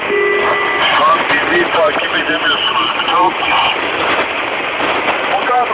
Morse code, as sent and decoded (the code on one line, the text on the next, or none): none